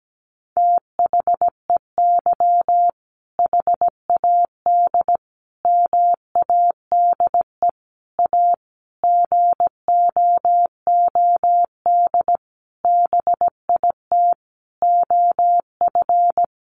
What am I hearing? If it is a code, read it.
Morse code, heard as a keyed tone, about 17 wpm, 700 Hz. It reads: THEY HAD MADE A GOOD BIT OF
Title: Sentences from Top 900 Words 17wpm